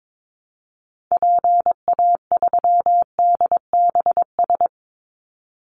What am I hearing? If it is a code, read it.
Morse code, heard as a keyed tone, about 22 wpm, 700 Hz. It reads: PA3DBS
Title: Call Signs - 22wpm